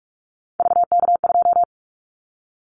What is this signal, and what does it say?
Morse code, heard as a keyed tone, about 46 wpm, 700 Hz. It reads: VK2
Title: Callsign Prefixes for Top 100 DXCC - 45wpm